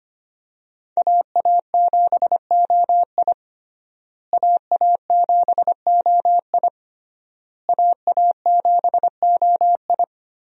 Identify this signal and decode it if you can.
Morse code, heard as a keyed tone, about 25 wpm, 700 Hz. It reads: AA7OI AA7OI AA7OI
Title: USA Call Signs - 5 Characters - Repeated 3x - 25wpm